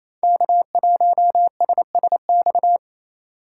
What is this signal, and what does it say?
Morse code, heard as a keyed tone, about 27 wpm, 700 Hz. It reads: K1SSX
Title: Code Groups - 5 Letter:Numbers - 28wpm